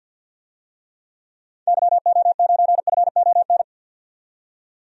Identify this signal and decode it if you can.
Morse code, heard as a keyed tone, about 50 wpm, 700 Hz. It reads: YO9FON
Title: DXCC-100 Call Signs - 6 Characters - 50wpm